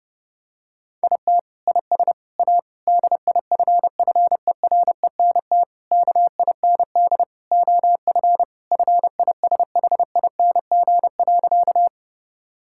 Morse code, heard as a keyed tone, about 32 wpm, 700 Hz. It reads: IT IS A DIFFERENT KIND OF FISHING.
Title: Sentences from Top 1000 Words - Review Part 5 - 30wpm